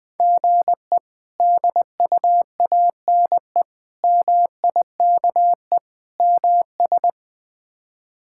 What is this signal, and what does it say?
Morse code, heard as a keyed tone, about 21 wpm, 700 Hz. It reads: GE DUANE MIKE MS
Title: K1USN SST QSO Practice - 20wpm